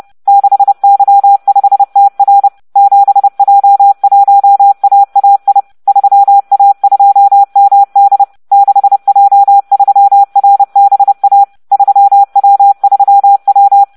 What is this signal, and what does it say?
Morse code, heard as a keyed tone, about 30 wpm, 800 Hz. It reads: 6Y5TR 7J1AAI 3A2MD 6J3RBA 3W3W